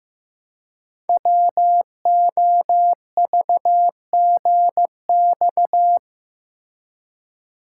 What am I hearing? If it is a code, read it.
Morse code, heard as a keyed tone, about 15 wpm, 700 Hz. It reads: WOVGX
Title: Code Groups - 5 Letters - 15wpm